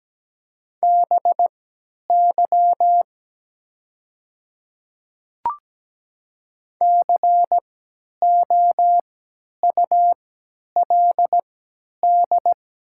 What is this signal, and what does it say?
Morse code, heard as a keyed tone, about 17 wpm, spaced as for 10 wpm, 700 Hz. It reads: BY COULD
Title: 3x Character Spacing - Top 100 Words - 17wpm